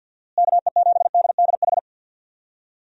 Morse code, heard as a keyed tone, about 47 wpm, 700 Hz. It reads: KE7DDH
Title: USA Call Signs - 6 Characters - Repeated 3x - 50wpm